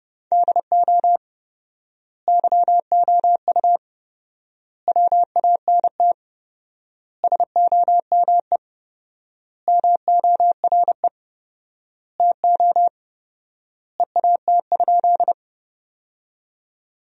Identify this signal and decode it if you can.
Morse code, heard as a keyed tone, about 30 wpm, 700 Hz. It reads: DO YOU WANT SOME MORE TO EAT?
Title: 4x Word-Spacing - Sentences from Top 300 Words - 30wpm